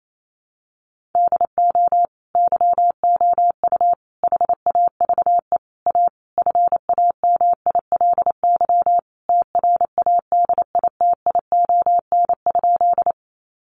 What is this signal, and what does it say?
Morse code, heard as a keyed tone, about 28 wpm, 700 Hz. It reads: DO YOU HAVE A FAMILY TRADITION?
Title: Sentences from Top 1500 Words 28wpm